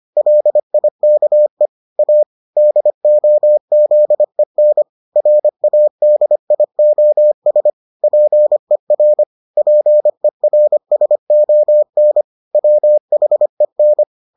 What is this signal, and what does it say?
Morse code, heard as a keyed tone, about 25 wpm, 600 Hz. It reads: LIKE A DOZEN RADIOS PER PERSON WHEN